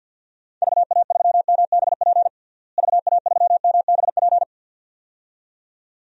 Morse code, heard as a keyed tone, about 48 wpm, 700 Hz. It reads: VA3MBP VA3MBP
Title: DXCC-100 Call Signs - 6 Characters - Repeated 2x - 50wpm